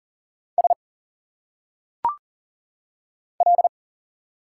Morse code, heard as a keyed tone, about 38 wpm, 700 Hz. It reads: S L
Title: Single Letters - Phonetic Pronunciation 40wpm